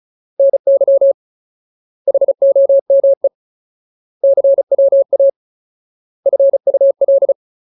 Morse code, heard as a keyed tone, about 34 wpm, 550 Hz. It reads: NY HOME CWA FUL
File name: Ultimate Word List / 35WPM Ultimate